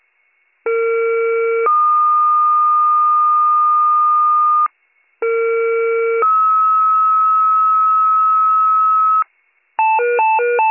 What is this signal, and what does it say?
Two-tone page, heard as a one-second tone, then a three-second tone, about 460 then 1220 Hz, about 460 then 1290 Hz. It is heard from 0.7 to 9.2 s.